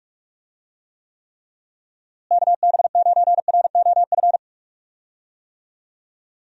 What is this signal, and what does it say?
Morse code, heard as a keyed tone, about 42 wpm, 700 Hz. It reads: KB9ROF
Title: USA Call Signs - 6 Characters - Repeated 3x - 45wpm